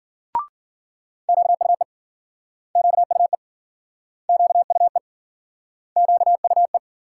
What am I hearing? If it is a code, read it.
Morse code, heard as a keyed tone, about 53 wpm, 700 Hz. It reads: QUE QUE QUE QUE